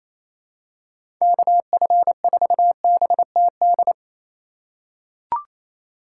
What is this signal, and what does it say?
Morse code, heard as a keyed tone, about 28 wpm, 700 Hz. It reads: KF4BTD
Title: USA Call Signs - 2x Character Spacing - 28wpm